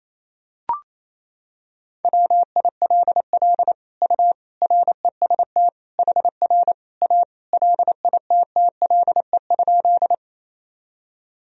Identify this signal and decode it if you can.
Morse code, heard as a keyed tone, about 28 wpm, 700 Hz. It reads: WILL U REST HR A LITTLE?
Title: Sentences from Top 500 Words and Abbreviations 28wpm